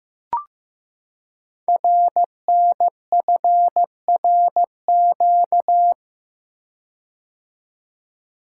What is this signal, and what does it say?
Morse code, heard as a keyed tone, about 15 wpm, 700 Hz. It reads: RNFRQ
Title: Code Groups - 5 Letters - 15wpm